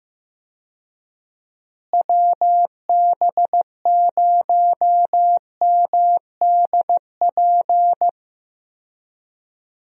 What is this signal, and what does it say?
Morse code, heard as a keyed tone, about 15 wpm, 700 Hz. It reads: WB0MDP
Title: USA Call Signs - 6 Characters - Repeated 3x - 15wpm